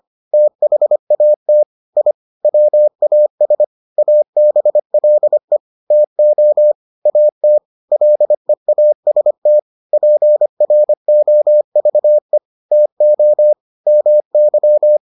Morse code, heard as a keyed tone, about 25 wpm, 600 Hz. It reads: THAT I WAS ABLE TO AT LEAST PROVE TO MY